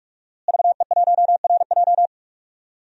Morse code, heard as a keyed tone, about 41 wpm, 700 Hz. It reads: VE1RJ
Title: DXCC-100 Call Signs - 5 Characters - Repeated 3x - 45wpm